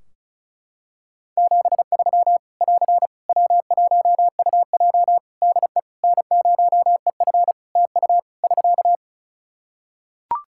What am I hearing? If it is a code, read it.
Morse code, heard as a keyed tone, about 34 wpm, 700 Hz. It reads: 73 <AR> W1UJ DE N0EF TU <SK>
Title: Practice QSO - Exchange 4 - Ending - 35wpm